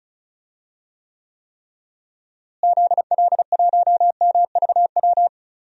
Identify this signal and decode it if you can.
Morse code, heard as a keyed tone, about 34 wpm, 700 Hz. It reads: ZL1MVW
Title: DXCC-100 Call Signs - 6 Characters - Repeated 3x - 35wpm